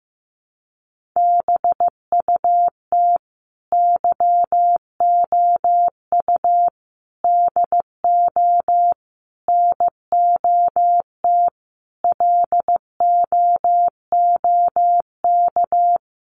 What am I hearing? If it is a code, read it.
Morse code, heard as a keyed tone, about 15 wpm, 700 Hz. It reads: BUT YOU DO NOT LOOK
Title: Sentences from Top 1100 Words 15wpm